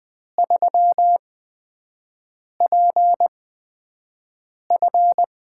Morse code, heard as a keyed tone, about 20 wpm, spaced as for 5 wpm, 700 Hz. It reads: 3PF
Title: USA Call Signs - 6 Characters - 8x Character Spacing - 20wpm